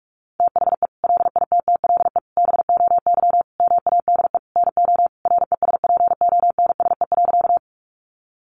Morse code, heard as a keyed tone, about 44 wpm, 700 Hz. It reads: THE LITTLE BOY MADE NO RESPONSE.
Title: Sentences from Top 1400 Words 45wpm